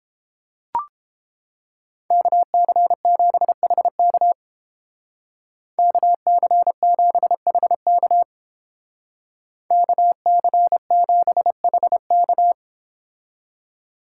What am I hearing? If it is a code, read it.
Morse code, heard as a keyed tone, about 30 wpm, 700 Hz. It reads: KC7HK KC7HK KC7HK